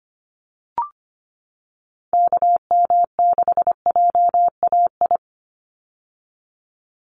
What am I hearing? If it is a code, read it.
Morse code, heard as a keyed tone, about 25 wpm, 700 Hz. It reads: KM6JAI
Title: USA Call Signs 25wpm